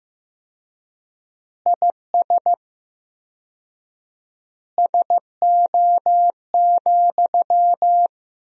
Morse code, heard as a keyed tone, about 15 wpm, 700 Hz. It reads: IS SO,
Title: 4x Word-Spacing - Sentences from Top 200 Words - 15wpm